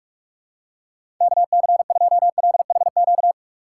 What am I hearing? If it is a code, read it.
Morse code, heard as a keyed tone, about 46 wpm, 700 Hz. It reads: KC2LHQ
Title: USA Call Signs - 6 Characters - 6x Character Spacing - 45wpm